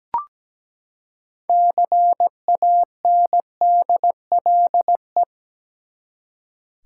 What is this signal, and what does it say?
Morse code, heard as a keyed tone, about 17 wpm, 700 Hz. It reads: CANDLE
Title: Six Letter Words - 17wpm